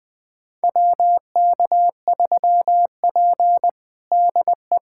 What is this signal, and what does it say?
Morse code, heard as a keyed tone, about 20 wpm, 700 Hz. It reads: WK3P DE
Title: USA Callsign DE Callsign - 4 Characters - 20wpm